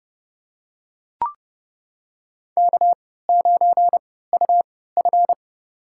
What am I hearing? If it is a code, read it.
Morse code, heard as a keyed tone, about 30 wpm, spaced as for 17 wpm, 700 Hz. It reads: K9UF
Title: USA Call Signs - 4 Characters - 3x Character Spacing - 30wpm